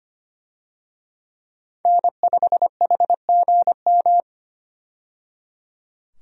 Morse code, heard as a keyed tone, about 25 wpm, 700 Hz. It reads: N5HGM
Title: USA Call Signs - 5 Characters - 25wpm